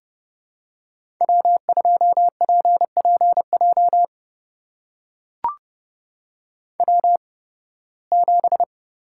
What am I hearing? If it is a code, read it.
Morse code, heard as a keyed tone, about 32 wpm, spaced as for 20 wpm, 700 Hz. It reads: W2PPJ W7
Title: USA Call Signs - 8x Character Spacing - 30wpm